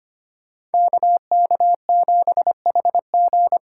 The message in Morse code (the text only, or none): KK7HG